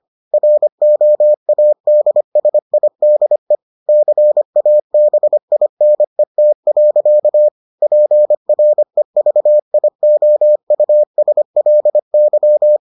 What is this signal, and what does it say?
Morse code, heard as a keyed tone, about 25 wpm, 600 Hz. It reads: ROADSIDE CABINET. PREVIOUSLY